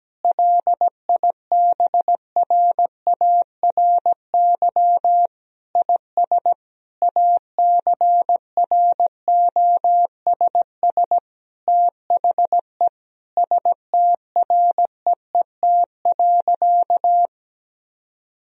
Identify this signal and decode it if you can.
Morse code, heard as a keyed tone, about 17 wpm, 700 Hz. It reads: LIBRARY IS ACROSS THE STREET.
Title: Sentences from Top 1000 Words - Review Part 4 - 17wpm